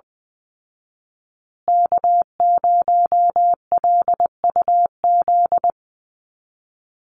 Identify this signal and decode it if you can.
Morse code, heard as a keyed tone, about 20 wpm, 700 Hz. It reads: K0LUZ
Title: Call Signs Repeated 2x - 20wpm